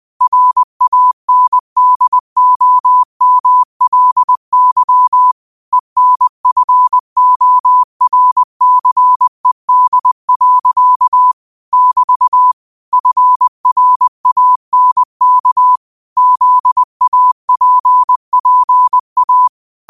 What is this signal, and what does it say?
Morse code, heard as a keyed tone, about 20 wpm, 1000 Hz. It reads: RANDOMLY ENFORCED. <BT> FRANK ZAPPA